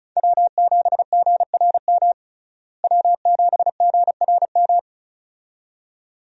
Morse code, heard as a keyed tone, about 34 wpm, 700 Hz. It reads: W7GRM W7GRM
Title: USA Call Signs - 5 Characters - Repeated 2x - 35wpm